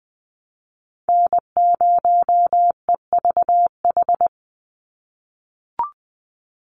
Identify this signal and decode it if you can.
Morse code, heard as a keyed tone, about 20 wpm, 700 Hz. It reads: N0EVH
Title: NA SOTA Chaser Callsigns 20wpm